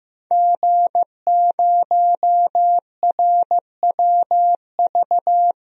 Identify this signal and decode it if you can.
Morse code, heard as a keyed tone, about 15 wpm, 700 Hz. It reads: G0RWV
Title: DXCC-100 Call Signs - 5 Characters - 15wpm